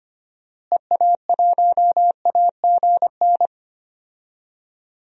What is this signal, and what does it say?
Morse code, heard as a keyed tone, about 25 wpm, 700 Hz. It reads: EA1AGN
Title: DXCC-100 Call Signs - 6 Characters - Repeated 3x - 25wpm